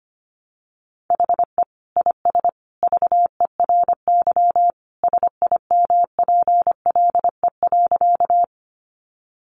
Morse code, heard as a keyed tone, about 25 wpm, 700 Hz. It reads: HE IS VERY SIMPLE.
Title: Sentences from Top 400 Words 25wpm